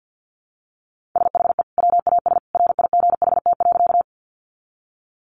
Morse code, heard as a keyed tone, about 50 wpm, 700 Hz. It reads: SHE WAS RIGHT.
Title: Sentences from Top 200 Words 50wpm